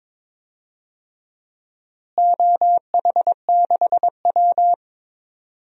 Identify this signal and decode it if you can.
Morse code, heard as a keyed tone, about 22 wpm, 700 Hz. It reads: OH6W
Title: DXCC-100 Call Signs - 4 Characters - Repeated 2x - 22wpm